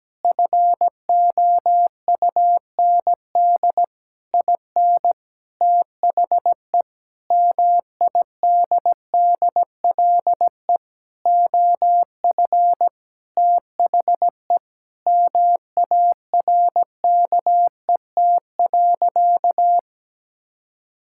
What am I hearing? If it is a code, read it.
Morse code, heard as a keyed tone, about 17 wpm, 700 Hz. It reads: FOUND IN THE MIDDLE OF THE MARKET.
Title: Sentences from Top 1000 Words - Review Part 6 - 17wpm